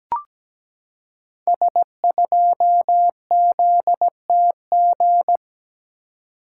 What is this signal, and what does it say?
Morse code, heard as a keyed tone, about 17 wpm, 700 Hz. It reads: S2ZTG
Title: Code Groups - 5 Letter:Numbers - 17wpm